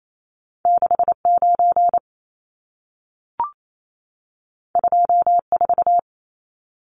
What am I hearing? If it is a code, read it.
Morse code, heard as a keyed tone, about 28 wpm, 700 Hz. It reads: 69 24
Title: Numbers - 2-Digit 28wpm